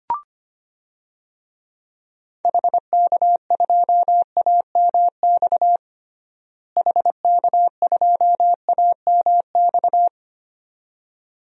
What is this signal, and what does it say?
Morse code, heard as a keyed tone, about 25 wpm, 700 Hz. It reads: HK2AMX HK2AMX
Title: DXCC-100 Call Signs - 6 Characters - Repeated 2x - 25wpm